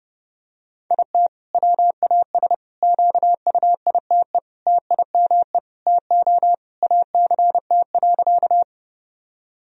Morse code, heard as a keyed tone, about 30 wpm, 700 Hz. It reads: IT WAS QUITE TIME TO ACT.
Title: Sentences from Top 1000 Words - Review Part 1 - 30wpm